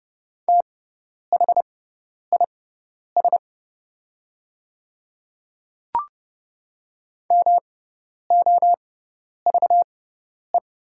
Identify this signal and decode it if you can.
Morse code, heard as a keyed tone, about 30 wpm, spaced as for 10 wpm, 690 Hz. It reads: THIS MOVE
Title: 6x Character Spacing - Top 300 Words - 30wpm